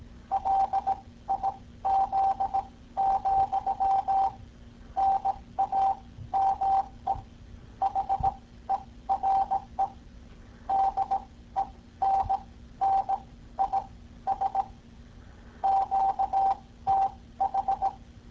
Morse code, heard as a keyed tone, about 18 wpm, spaced as for 14 wpm, 780 Hz. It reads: LIZ, NAME HERE DENNIS QTH